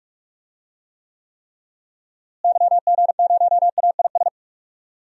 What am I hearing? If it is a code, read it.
Morse code, heard as a keyed tone, about 43 wpm, 700 Hz. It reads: YG0AIS
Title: DXCC-100 Call Signs - 6 Characters - Repeated 3x - 45wpm